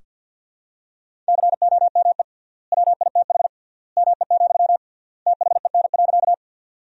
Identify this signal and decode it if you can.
Morse code, heard as a keyed tone, about 47 wpm, 700 Hz. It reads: COME WITH ME, THEN.
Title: 3x Word-Spacing - Sentences from Top 200 Words - 50wpm